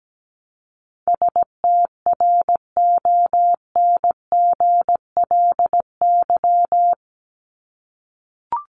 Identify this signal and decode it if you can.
Morse code, heard as a keyed tone, about 17 wpm, 700 Hz. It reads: STRONGLY
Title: LY Suffix - Top 100 Words with Base Words 17wpm